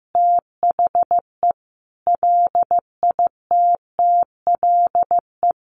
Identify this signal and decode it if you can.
Morse code, heard as a keyed tone, about 15 wpm, 700 Hz. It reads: THE LITTLE
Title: Sentences from Top 200 Words 15wpm